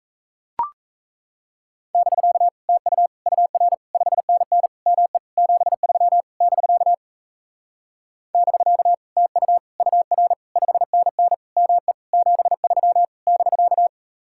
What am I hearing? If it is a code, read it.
Morse code, heard as a keyed tone, about 37 wpm, 700 Hz. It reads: <BK> TU UR 5NN ME 73 <BK> <BK> TU UR 5NN ME 73 <BK>